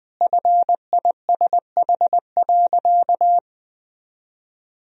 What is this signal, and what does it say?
Morse code, heard as a keyed tone, about 21 wpm, 700 Hz. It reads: FISH.
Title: Sentences from Top 1000 Words - Review Part 1 - 20wpm